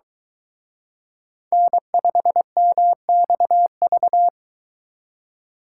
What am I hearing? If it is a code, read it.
Morse code, heard as a keyed tone, about 22 wpm, 700 Hz. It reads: N5MXV